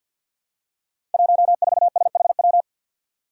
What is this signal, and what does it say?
Morse code, heard as a keyed tone, about 52 wpm, 700 Hz. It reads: 14SHW
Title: Five Challenging Letter-Number Combinations 50wpm